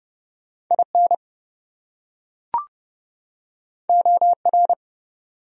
Morse code, heard as a keyed tone, about 30 wpm, 700 Hz. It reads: IN OR
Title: Top 100 Words 30wpm